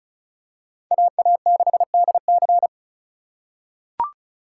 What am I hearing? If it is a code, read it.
Morse code, heard as a keyed tone, about 36 wpm, 700 Hz. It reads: AA6DC
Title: USA Call Signs - 5 Characters - 4x Character Spacing - 35wpm